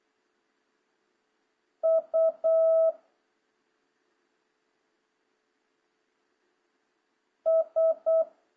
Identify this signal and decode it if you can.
Morse code, heard as a keyed tone, about 8 wpm, 1300 Hz. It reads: U S